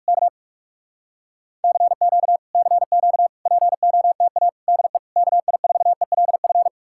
Morse code, heard as a keyed tone, about 67 wpm, spaced as for 56 wpm, 700 Hz. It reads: K CQ CQ POTA DE KI4ELF
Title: Speed-Racing Common QSO Phrases Encore - 45wpm